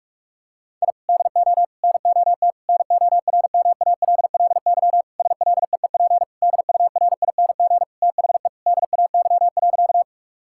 Morse code, heard as a keyed tone, about 46 wpm, 700 Hz. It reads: I DO NOT NORMALLY SLEEP DURING THE DAY.
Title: Sentences from Top 1000 Words - Review Part 6 - 45wpm